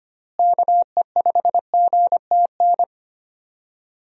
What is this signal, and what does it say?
Morse code, heard as a keyed tone, about 25 wpm, 700 Hz. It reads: KE5GTN
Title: USA Call Signs - 6 Characters - Repeated 2x - 25wpm